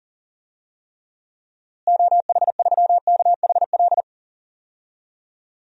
Morse code, heard as a keyed tone, about 40 wpm, 700 Hz. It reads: OH3KHL
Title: DXCC-100 Call Signs - 6 Characters - Repeated 3x - 40wpm